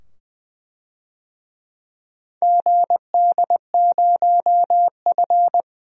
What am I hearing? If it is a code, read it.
Morse code, heard as a keyed tone, about 21 wpm, 700 Hz. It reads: GD0F
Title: DXCC-100 Call Signs - 4 Characters - 20wpm